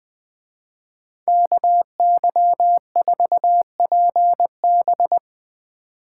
Morse code, heard as a keyed tone, about 20 wpm, 700 Hz. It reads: KY4PB